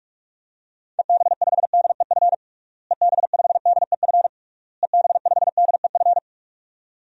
Morse code, heard as a keyed tone, about 46 wpm, 700 Hz. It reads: EB5DEF EB5DEF EB5DEF
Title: DXCC-100 Call Signs - 6 Characters - Repeated 3x - 45wpm